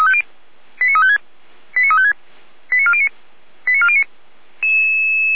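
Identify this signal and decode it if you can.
Five-tone selective call (ZVEI), heard as quick runs of five tones, tones of about 70 ms. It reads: 89356 89356 89409 89409
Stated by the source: ZVEI five-tone selective call received by SDR.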